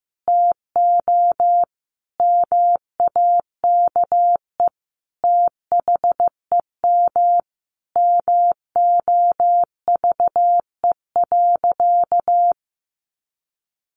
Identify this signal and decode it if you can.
Morse code, heard as a keyed tone, about 15 wpm, 700 Hz. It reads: TO MAKE THEM MOVE.
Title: A Sound Of Thunder 15wpm